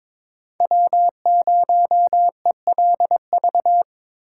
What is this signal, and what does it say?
Morse code, heard as a keyed tone, about 23 wpm, 700 Hz. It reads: W0ELV
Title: Code Groups - 5 Letter:Numbers - 22wpm